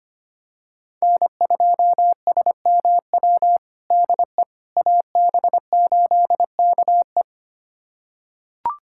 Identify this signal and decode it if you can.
Morse code, heard as a keyed tone, about 25 wpm, 700 Hz. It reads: N2SMW DE AB8KE
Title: USA Callsign DE Callsign - 5 Characters - 25wpm